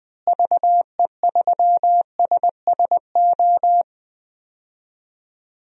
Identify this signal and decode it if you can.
Morse code, heard as a keyed tone, about 20 wpm, 700 Hz. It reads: VE3SSO
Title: DXCC-100 Call Signs - 6 Characters - Repeated 3x - 20wpm